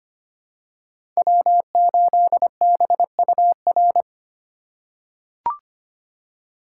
Morse code, heard as a keyed tone, about 25 wpm, 700 Hz. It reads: W8BUR
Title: USA Call Signs - 6x Character Spacing - 25wpm